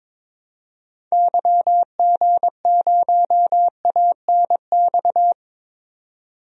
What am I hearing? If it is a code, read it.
Morse code, heard as a keyed tone, about 22 wpm, 700 Hz. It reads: YG0ANX
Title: DXCC-100 Call Signs - 6 Characters - 22wpm